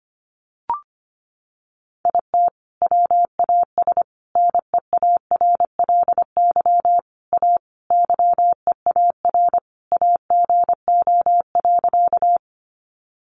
Morse code, heard as a keyed tone, about 25 wpm, 700 Hz. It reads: IT WAS NEARLY A YEAR AGO.
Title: Sentences from Top 1300 Words 25wpm